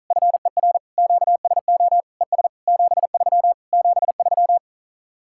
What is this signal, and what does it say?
Morse code, heard as a keyed tone, about 46 wpm, 700 Hz. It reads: FER QSO ES 73 73